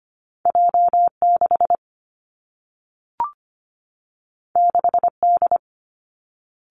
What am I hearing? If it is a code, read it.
Morse code, heard as a keyed tone, about 25 wpm, 700 Hz. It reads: J 6 6 D
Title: Two Challenging Letter-Number Combinations 25wpm